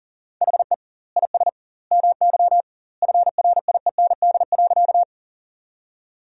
Morse code, heard as a keyed tone, about 40 wpm, 700 Hz. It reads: HE IS MY FRIEND.
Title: 2x Word-Spacing - Sentences from Top 300 Words - 40wpm